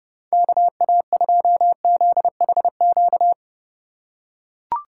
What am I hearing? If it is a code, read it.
Morse code, heard as a keyed tone, about 30 wpm, 700 Hz. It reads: KA2ZHQ
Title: USA Call Signs - 6 Characters - 2x Character Spacing - 30wpm